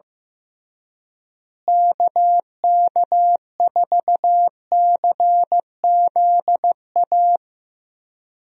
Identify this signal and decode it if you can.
Morse code, heard as a keyed tone, about 15 wpm, 700 Hz. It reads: KK4CZA
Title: USA Call Signs - 6 Characters - Repeated 3x - 15wpm